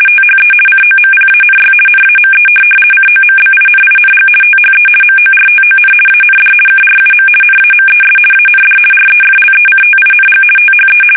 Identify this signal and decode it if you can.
RTTY, about 50 baud. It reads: THE COMPANY NAME COMBINING THE MORTON AND KRUM NAMES AND REFLECTING THE
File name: fsk850-50B / teletype-corp